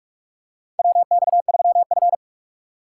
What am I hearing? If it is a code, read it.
Morse code, heard as a keyed tone, about 43 wpm, 700 Hz. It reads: WX3F
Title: USA Call Signs - 2x Character Spacing - 45wpm